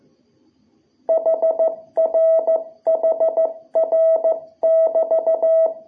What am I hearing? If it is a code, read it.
Morse code, heard as a keyed tone, about 15 wpm, 660 Hz. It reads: HR HR <BT>
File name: jh8cbh / 2021cw